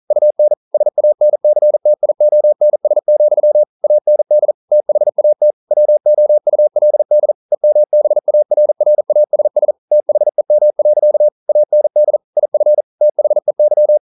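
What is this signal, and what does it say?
Morse code, heard as a keyed tone, about 42 wpm, 600 Hz. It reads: UN SANCTIONS, AND THAT WOULD EMBARRASS THEM. AND IF THEY
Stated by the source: Morse code practice at 41 wpm, from CW med Gnister.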